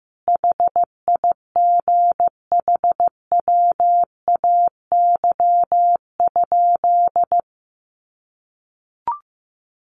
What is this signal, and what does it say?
Morse code, heard as a keyed tone, about 15 wpm, 700 Hz. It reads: HIGHWAY?
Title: Sentences from Top 1400 Words 15wpm